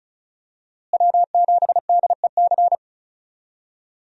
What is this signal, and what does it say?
Morse code, heard as a keyed tone, about 36 wpm, 700 Hz. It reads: W7DEC
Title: USA Call Signs - 5 Characters - Repeated 2x - 35wpm